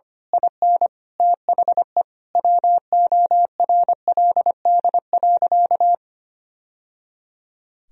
Morse code, heard as a keyed tone, about 25 wpm, 700 Hz. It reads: IN THE WORLD.
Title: Sentences from Top 1000 Words - Review Part 3 - 25wpm